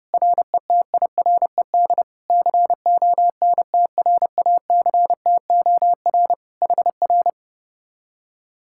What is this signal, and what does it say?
Morse code, heard as a keyed tone, about 31 wpm, 700 Hz. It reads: RETIRED CONTRACTOR HR